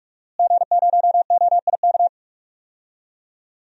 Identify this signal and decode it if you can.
Morse code, heard as a keyed tone, about 47 wpm, 700 Hz. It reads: G0OIK
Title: DXCC-100 Call Signs - 5 Characters - Repeated 2x - 45wpm